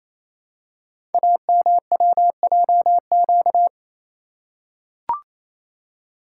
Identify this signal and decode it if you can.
Morse code, heard as a keyed tone, about 29 wpm, 700 Hz. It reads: AMWJQ
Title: Code Groups - 5 Letters - 28wpm